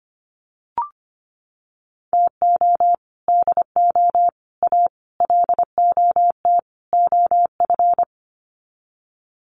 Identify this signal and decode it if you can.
Morse code, heard as a keyed tone, about 25 wpm, 700 Hz. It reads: TO DO A LOT OF
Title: Sets of 5 Words Encore 25wpm